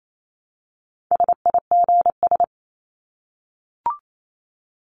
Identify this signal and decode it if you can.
Morse code, heard as a keyed tone, about 28 wpm, 700 Hz. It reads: SIGS
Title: Abbreviations 28wpm